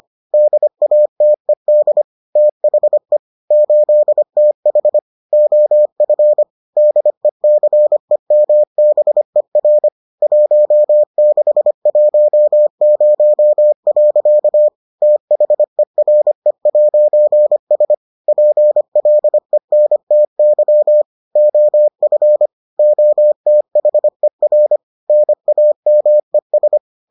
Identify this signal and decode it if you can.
Morse code, heard as a keyed tone, about 25 wpm, 600 Hz. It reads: DATED THE 8TH OF DECEMBER 1610. THERE'S PLENTY OF OTHER NAMES